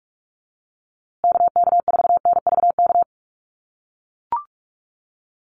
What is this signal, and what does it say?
Morse code, heard as a keyed tone, about 44 wpm, 700 Hz. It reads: KK4NVK
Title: USA Call Signs Repeated 2x 45wpm